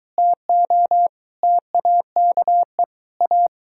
Morse code, heard as a keyed tone, about 22 wpm, 700 Hz. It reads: TO TAKE A